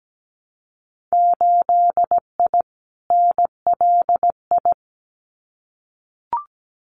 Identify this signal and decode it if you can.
Morse code, heard as a keyed tone, about 17 wpm, 700 Hz. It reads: 8I NLI
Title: Winter Field Day Exchanges Encore 17wpm